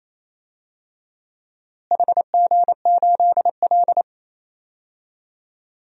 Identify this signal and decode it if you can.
Morse code, heard as a keyed tone, about 28 wpm, 700 Hz. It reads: HG8L
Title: DXCC-100 Call Signs - 4 Characters - Repeated 3x - 28wpm